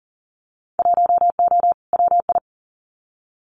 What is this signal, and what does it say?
Morse code, heard as a keyed tone, about 38 wpm, 700 Hz. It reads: 1O WI
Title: Winter Field Day Exchanges 40wpm